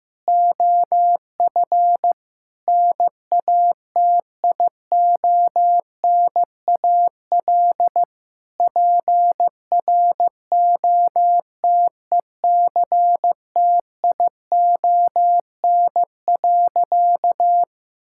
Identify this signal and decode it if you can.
Morse code, heard as a keyed tone, about 16 wpm, 700 Hz. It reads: OF NATIONAL PROTECTION.
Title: Sentences from Top 1600 Words 15wpm